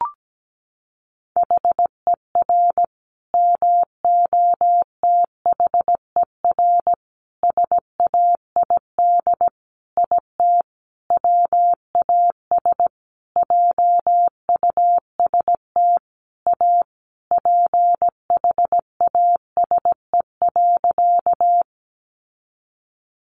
Morse code, heard as a keyed tone, about 17 wpm, 700 Hz. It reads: HER MOTHER SAID IT WAS JUST A PHASE.
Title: Sentences from Top 1200 Words 17wpm